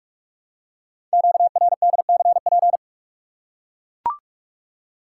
Morse code, heard as a keyed tone, about 44 wpm, 700 Hz. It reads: QRDCP
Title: Code Groups - 5 Letter:Numbers - 45wpm